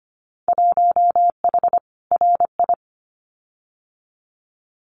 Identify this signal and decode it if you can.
Morse code, heard as a keyed tone, about 25 wpm, 700 Hz. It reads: 1H RI
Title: Winter Field Day Exchanges 25wpm